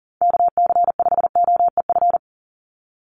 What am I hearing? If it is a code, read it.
Morse code, heard as a keyed tone, about 40 wpm, 700 Hz. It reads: KC5OEF
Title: USA Call Signs 40wpm